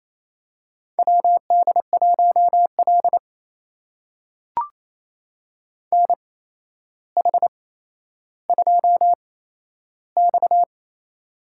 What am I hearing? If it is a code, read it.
Morse code, heard as a keyed tone, about 28 wpm, spaced as for 12 wpm, 700 Hz. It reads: WD1L NH2X
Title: USA Call Signs - 4 Characters - 8x Character Spacing - 28wpm